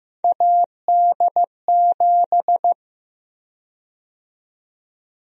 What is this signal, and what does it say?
Morse code, heard as a keyed tone, about 15 wpm, 700 Hz. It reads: AD7
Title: USA Call Sign Prefixes - Repeated 2x - 15wpm